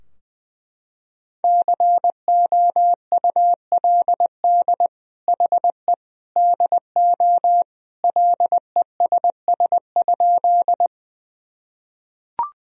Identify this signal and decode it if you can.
Morse code, heard as a keyed tone, about 21 wpm, 700 Hz. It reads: COULD HE DO LESS?
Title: Sentences from Top 1000 Words - Review Part 2 - 20wpm